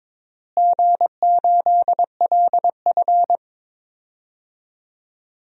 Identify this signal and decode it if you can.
Morse code, heard as a keyed tone, about 22 wpm, 700 Hz. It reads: G8LF
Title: DXCC-100 Call Signs - 4 Characters - Repeated 3x - 22wpm